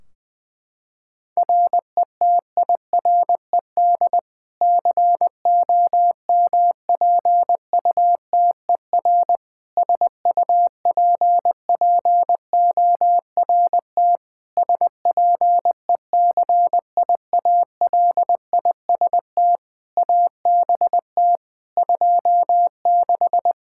Morse code, heard as a keyed tone, about 21 wpm, 700 Hz. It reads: RETIRED COMPUTER SUPPORT SPECIALIST ABT 26
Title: Practice QSO - Exchange 3 - More Info - 20wpm